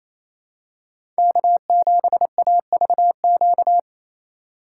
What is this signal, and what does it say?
Morse code, heard as a keyed tone, about 28 wpm, 700 Hz. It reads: K7AVQ
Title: USA Call Signs - 5 Characters - 2x Character Spacing - 28wpm